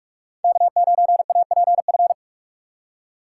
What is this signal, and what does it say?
Morse code, heard as a keyed tone, about 44 wpm, 700 Hz. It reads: K9APF